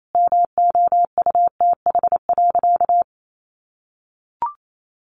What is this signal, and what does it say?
Morse code, heard as a keyed tone, about 28 wpm, 700 Hz. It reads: MOUTH.
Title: Sentences from Top 1500 Words 28wpm